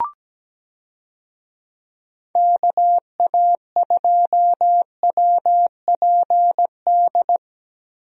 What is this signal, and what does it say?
Morse code, heard as a keyed tone, about 17 wpm, 700 Hz. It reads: KA2WPD
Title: USA Call Signs - 6 Characters - 17wpm